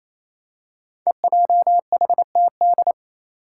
Morse code, heard as a keyed tone, about 28 wpm, 700 Hz. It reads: EJHTD
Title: Five Challenging Letter-Number Combinations 28wpm